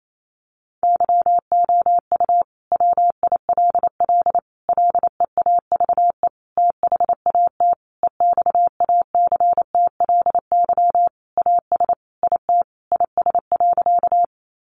Morse code, heard as a keyed tone, about 28 wpm, 700 Hz. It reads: YOU WILL LEAVE THAT EXACTLY AS IT IS.
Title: Sentences from Top 900 Words 28wpm